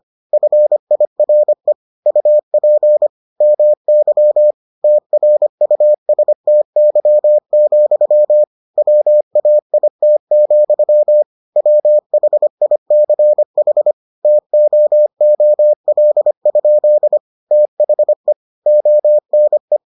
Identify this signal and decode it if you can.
Morse code, heard as a keyed tone, about 25 wpm, 600 Hz. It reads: FIRE UP MY TRUSTY, WAIT, WHICH TOOL? THE ONE